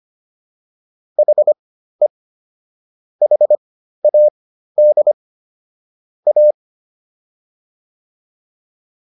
Morse code, heard as a keyed tone, about 25 wpm, spaced as for 13 wpm, 600 Hz. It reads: HE HAD A